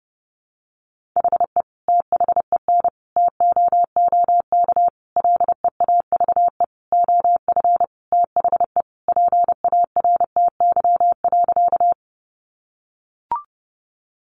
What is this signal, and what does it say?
Morse code, heard as a keyed tone, about 30 wpm, 700 Hz. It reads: HE THEN TOOK LEAVE OF THE PARTY.
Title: Sentences from Top 1400 Words 30wpm